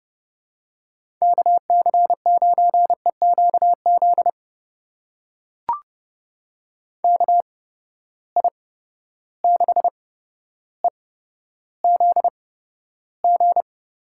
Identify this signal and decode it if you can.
Morse code, heard as a keyed tone, about 30 wpm, spaced as for 13 wpm, 700 Hz. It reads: KC9EQZ KI6EZG
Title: USA Call Signs - 6 Characters - 8x Character Spacing - 30wpm